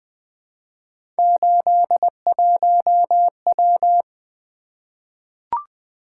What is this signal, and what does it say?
Morse code, heard as a keyed tone, about 20 wpm, 700 Hz. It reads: 81W
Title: Three Letter-Number Combinations - 20wpm